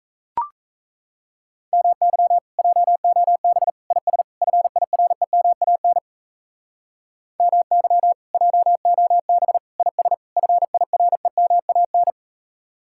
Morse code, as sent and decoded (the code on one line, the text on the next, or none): -- -.-- / .--- --- -... / .. ... / ..-. .. .-. -- .- -. / -- -.-- / .--- --- -... / .. ... / ..-. .. .-. -- .- -.
MY JOB IS FIREMAN MY JOB IS FIREMAN